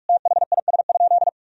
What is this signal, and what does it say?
Morse code, heard as a keyed tone, about 46 wpm, 700 Hz. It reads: THIS?
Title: Sentences from Top 1000 Words - Review Part 2 - 45wpm